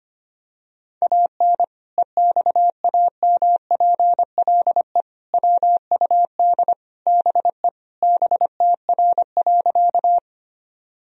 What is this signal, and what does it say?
Morse code, heard as a keyed tone, about 25 wpm, 700 Hz. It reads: AN EXAMPLE WUD BE BTR.
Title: Sentences from Top 400 Words and Abbreviations 25wpm